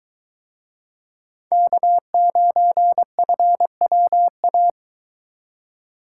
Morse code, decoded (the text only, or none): K9FWA